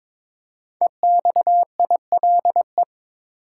Morse code, heard as a keyed tone, about 22 wpm, 700 Hz. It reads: EXILE
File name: 5 Letter Words - First 3 Letters Given - 22wpm